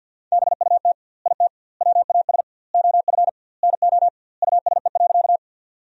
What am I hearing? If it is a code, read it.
Morse code, heard as a keyed tone, about 52 wpm, 700 Hz. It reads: BUT IT WAS OF NO USE.
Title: 2x Word-Spacing - Sentences from Top 100 Words - 50wpm